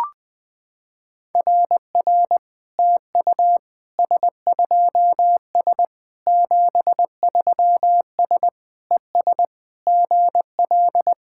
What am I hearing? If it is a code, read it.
Morse code, heard as a keyed tone, about 20 wpm, 700 Hz. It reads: RR TU S2S 73S ES GL
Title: SOTA QSO Practice 20wpm